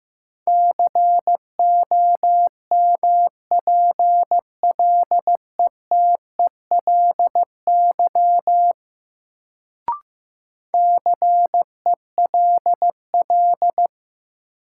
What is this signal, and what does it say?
Morse code, heard as a keyed tone, about 15 wpm, 700 Hz. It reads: COMPLETELY CELL
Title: Sentences from Top 1100 Words - New Word Drill Alt - 15wpm